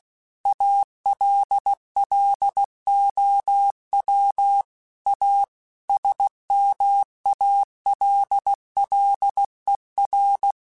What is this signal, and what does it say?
Morse code, heard as a keyed tone, about 16 wpm, 800 Hz. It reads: ALLOW A SMALLER
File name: Technical Subjects / Technical Subjects-mp3s / Sw Pwr Sup @15